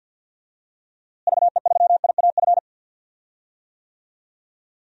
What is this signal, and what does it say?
Morse code, heard as a keyed tone, about 47 wpm, 700 Hz. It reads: VE3IAF
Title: DXCC-100 Call Signs - 6 Characters - Repeated 2x - 50wpm